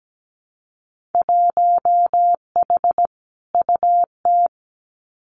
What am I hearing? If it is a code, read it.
Morse code, heard as a keyed tone, about 17 wpm, 700 Hz. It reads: 1H UT
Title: Winter Field Day Exchanges 17wpm